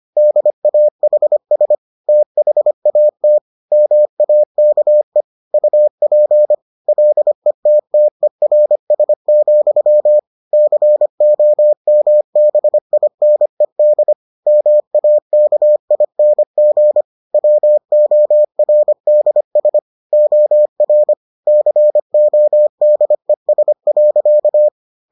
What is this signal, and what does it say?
Morse code, heard as a keyed tone, about 25 wpm, 600 Hz. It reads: DAHS THAT MAKE UP LETTERS, COMBINED MAKING WORDS OR CODES.